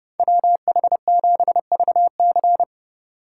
Morse code, heard as a keyed tone, about 30 wpm, 700 Hz. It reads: WH7VC